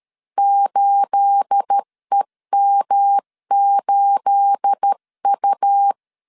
Morse code, heard as a keyed tone, about 13 wpm, 800 Hz. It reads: 8EM8U
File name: LEZIONI / 9 286 12wpm 2305